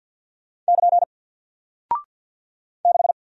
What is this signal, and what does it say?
Morse code, heard as a keyed tone, about 48 wpm, 700 Hz. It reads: <KN> 6
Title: Single Character (Letter:Number:Punctuation:Prosign) Alt 50wpm